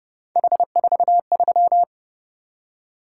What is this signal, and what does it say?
Morse code, heard as a keyed tone, about 29 wpm, 700 Hz. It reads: H43